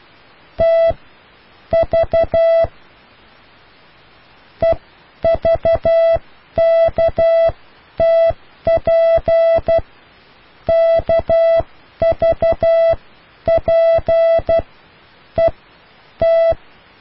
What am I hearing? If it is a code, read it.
Morse code, heard as a keyed tone, about 12 wpm, spaced as for 10 wpm, 660 Hz. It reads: T V EVKTP KVP E T